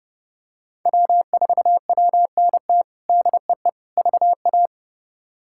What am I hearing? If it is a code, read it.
Morse code, heard as a keyed tone, about 30 wpm, 700 Hz. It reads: W4WNT DEE VA
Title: K1USN SST QSO Practice - 30wpm